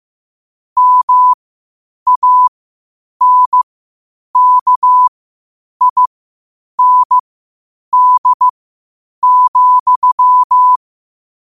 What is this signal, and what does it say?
Morse code, heard as a keyed tone, about 15 wpm, spaced as for 8 wpm, 1000 Hz. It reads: MANKIND,
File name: qotdfiles / 8wpm